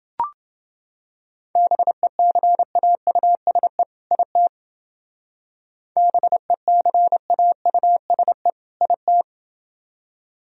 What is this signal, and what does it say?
Morse code, heard as a keyed tone, about 30 wpm, 700 Hz. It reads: BECAUSE IT BECAUSE IT